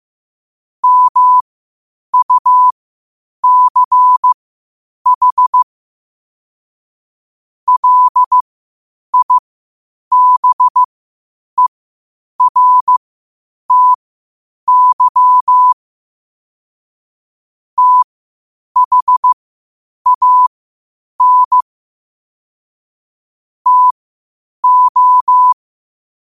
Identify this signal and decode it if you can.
Morse code, heard as a keyed tone, about 15 wpm, spaced as for 8 wpm, 1000 Hz. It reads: MUCH LIBERTY THAN TO
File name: qotdfiles / 8wpm